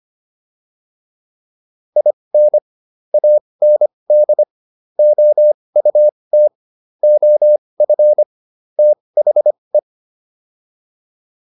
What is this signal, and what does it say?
Morse code, heard as a keyed tone, about 25 wpm, spaced as for 20 wpm, 600 Hz. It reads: IN AND OUT OF THE
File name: WD405 20